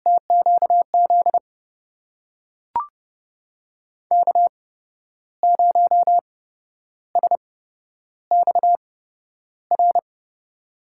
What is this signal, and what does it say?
Morse code, heard as a keyed tone, about 30 wpm, spaced as for 11 wpm, 700 Hz. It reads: TQZ K0SXR
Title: USA Call Signs - 5 Characters - 8x Character Spacing - 30wpm